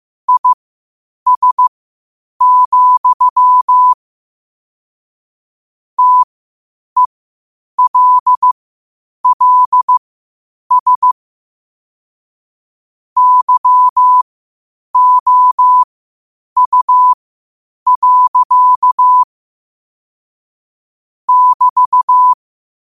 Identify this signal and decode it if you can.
Morse code, heard as a keyed tone, about 15 wpm, spaced as for 8 wpm, 1000 Hz. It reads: IS, TELLS YOU. <BT>
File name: 8wpm